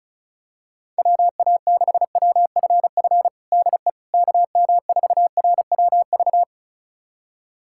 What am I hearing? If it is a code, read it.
Morse code, heard as a keyed tone, about 36 wpm, 700 Hz. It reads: WA6WFF DE KM4RWV
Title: USA Callsign DE Callsign 35wpm